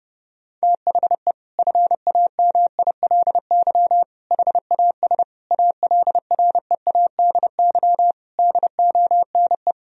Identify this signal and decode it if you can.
Morse code, heard as a keyed tone, about 32 wpm, 700 Hz. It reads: THE FAMILY HAS ALREADY DONE
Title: Sentences from Top 1000 Words - Review Part 2 - 30wpm